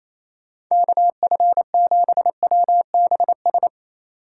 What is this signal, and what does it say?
Morse code, heard as a keyed tone, about 34 wpm, 700 Hz. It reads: KF7WBS
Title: Speed Racing - USA 2x3 Call Signs 28wpm